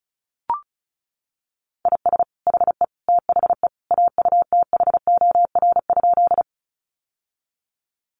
Morse code, heard as a keyed tone, about 34 wpm, 700 Hz. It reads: IS HE THE AUTHOR?
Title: Sentences from Top 1200 Words 35wpm